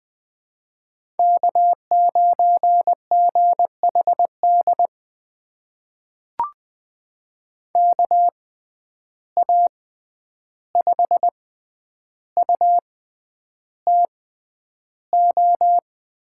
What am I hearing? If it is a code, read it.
Morse code, heard as a keyed tone, about 20 wpm, spaced as for 10 wpm, 700 Hz. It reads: K9GHD KA5UTO
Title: USA Call Signs - 6x Character Spacing - 20wpm